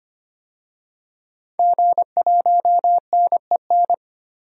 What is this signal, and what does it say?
Morse code, heard as a keyed tone, about 25 wpm, 700 Hz. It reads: G1NEN
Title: DXCC-100 Call Signs - 5 Characters - 25wpm